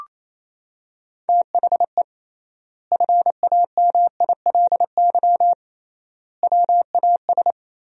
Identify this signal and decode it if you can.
Morse code, heard as a keyed tone, about 28 wpm, 700 Hz. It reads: THE FAMILY WAS
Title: 3x Word-Spacing - Sentences from Top 400 Words - 28wpm